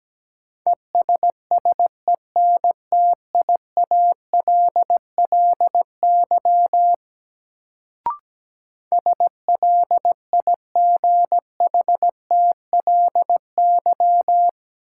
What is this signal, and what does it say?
Morse code, heard as a keyed tone, about 17 wpm, 700 Hz. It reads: ESSENTIALLY SLIGHTLY
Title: Sentences from Top 1300 Words - New Word Drill - 17wpm